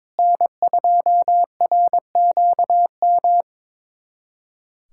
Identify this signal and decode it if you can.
Morse code, heard as a keyed tone, about 22 wpm, 700 Hz. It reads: N2RQM
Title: USA Callsign DE Callsign - 5 Characters - 22wpm